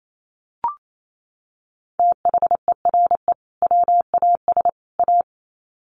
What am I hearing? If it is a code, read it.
Morse code, heard as a keyed tone, about 28 wpm, 700 Hz. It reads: THERE WAS A
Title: Sets of 3 Words 28wpm